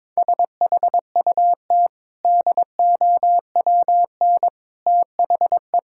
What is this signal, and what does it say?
Morse code, heard as a keyed tone, about 22 wpm, 690 Hz. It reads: SHUT DOWN THE
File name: practice / 22wpm / Easy Sentences with Hints - 22wpm